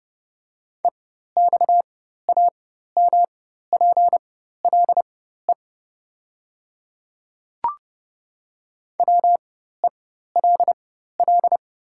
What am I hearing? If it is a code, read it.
Morse code, heard as a keyed tone, about 30 wpm, spaced as for 14 wpm, 700 Hz. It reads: EXAMPLE WELL
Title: 4x Character Spacing - Top 300 Words - 30wpm